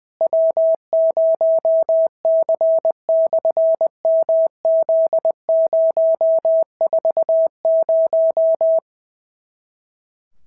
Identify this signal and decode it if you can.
Morse code, heard as a keyed tone, about 20 wpm, 650 Hz. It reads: W0C/MZ040
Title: SOTA QSO Practice 20wpm